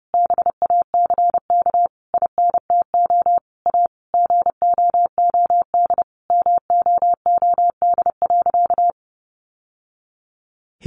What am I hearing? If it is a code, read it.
Morse code, heard as a keyed tone, about 28 wpm, 700 Hz. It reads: BACK INTO A GOOD MOOD.
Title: Sentences from Top 1100 Words 30wpm